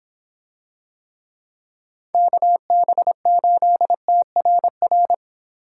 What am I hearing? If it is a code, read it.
Morse code, heard as a keyed tone, about 26 wpm, 700 Hz. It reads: KB8TRR